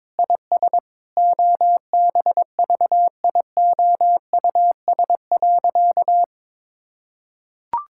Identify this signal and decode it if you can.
Morse code, heard as a keyed tone, about 22 wpm, 700 Hz. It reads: IS OBVIOUS.
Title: Sentences from Top 1600 Words 22wpm